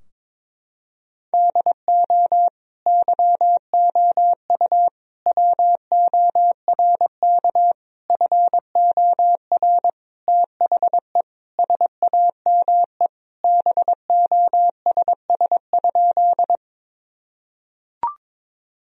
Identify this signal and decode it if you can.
Morse code, heard as a keyed tone, about 22 wpm, 700 Hz. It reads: DO YOU WORK FOR THE SAME BOSS?
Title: Sentences from Top 1000 Words - Review Part 4 - 22wpm